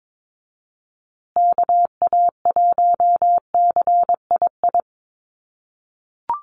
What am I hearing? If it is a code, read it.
Morse code, heard as a keyed tone, about 22 wpm, 700 Hz. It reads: KA1CII
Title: USA Call Signs Repeated 3x 22wpm